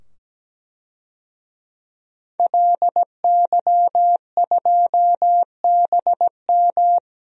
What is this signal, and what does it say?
Morse code, heard as a keyed tone, about 17 wpm, 700 Hz. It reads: LY2BM
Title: DXCC-100 Call Signs - 5 Characters - 17wpm